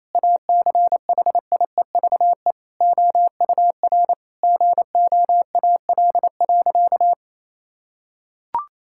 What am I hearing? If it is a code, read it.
Morse code, heard as a keyed tone, about 28 wpm, 700 Hz. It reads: ACHIEVE OUR GOAL.
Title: Sentences from Top 1000 Words - Review Part 6 - 28wpm